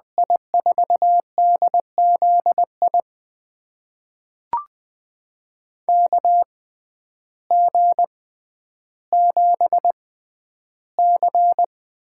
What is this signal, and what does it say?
Morse code, heard as a keyed tone, about 20 wpm, spaced as for 11 wpm, 700 Hz. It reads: I4DZI KG7C
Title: USA Call Signs - 6 Characters - 6x Character Spacing - 20wpm